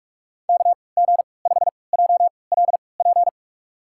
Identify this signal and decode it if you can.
Morse code, heard as a keyed tone, about 46 wpm, spaced as for 25 wpm, 700 Hz. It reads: KG5JLP
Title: USA Call Signs - 6 Characters - 3x Character Spacing - 45wpm